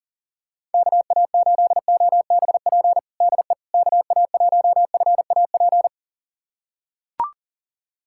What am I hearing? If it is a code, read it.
Morse code, heard as a keyed tone, about 40 wpm, 700 Hz. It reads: KA8OBP DE KA1FAP
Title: USA Callsign DE Callsign - 6 Characters - 40wpm